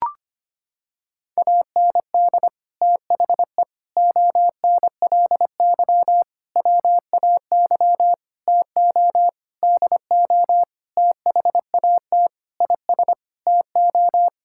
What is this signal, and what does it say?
Morse code, heard as a keyed tone, about 25 wpm, 700 Hz. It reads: AND THE ONLY WAY TO DO THAT IS TO
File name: practice / 25wpm / Sentences with 9-Word Phrases - 25wpm